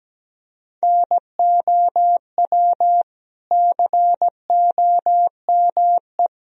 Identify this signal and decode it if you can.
Morse code, heard as a keyed tone, about 17 wpm, 700 Hz. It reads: NOW COME
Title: Sentences from Top 200 Words and Abbreviations 17wpm